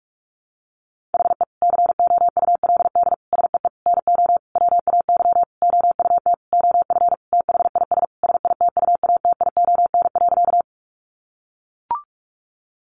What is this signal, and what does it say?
Morse code, heard as a keyed tone, about 44 wpm, 700 Hz. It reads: HE COULD SEE NO WAY OUT OF THIS SITUATION.
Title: Sentences from Top 1200 Words 45wpm